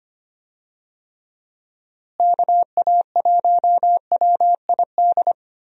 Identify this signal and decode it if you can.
Morse code, heard as a keyed tone, about 25 wpm, 700 Hz. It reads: KA1WID
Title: USA Call Signs - 6 Characters - Repeated 3x - 25wpm